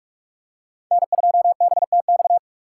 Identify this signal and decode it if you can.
Morse code, heard as a keyed tone, about 46 wpm, 700 Hz. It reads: N2BTX
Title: K1USN SST QSO Practice - 45wpm